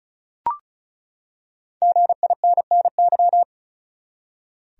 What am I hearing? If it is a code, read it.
Morse code, heard as a keyed tone, about 36 wpm, 700 Hz. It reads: GINNY
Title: USA Common Names - Max 5 Letters - 35wpm